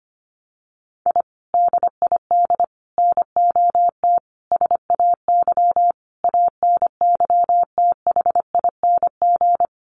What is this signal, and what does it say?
Morse code, heard as a keyed tone, about 25 wpm, 700 Hz. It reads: I DID NOT SAY ANYTHING
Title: Sets of 5 Words Encore2 25wpm